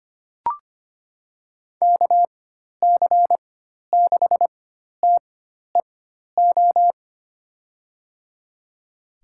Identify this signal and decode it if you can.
Morse code, heard as a keyed tone, about 25 wpm, spaced as for 12 wpm, 700 Hz. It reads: KC6TEO
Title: USA Call Signs - 4x Character Spacing - 25wpm